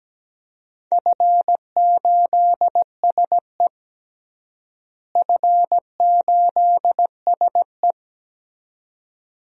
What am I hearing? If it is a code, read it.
Morse code, heard as a keyed tone, about 17 wpm, 700 Hz. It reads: F8SE F8SE
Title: DXCC-100 Call Signs - 4 Characters - Repeated 2x - 17wpm